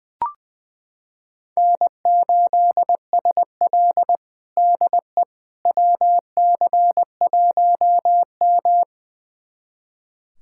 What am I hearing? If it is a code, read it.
Morse code, heard as a keyed tone, about 21 wpm, 700 Hz. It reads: N8SL DE WC1M
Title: USA Callsign DE Callsign - 4 Characters - 20wpm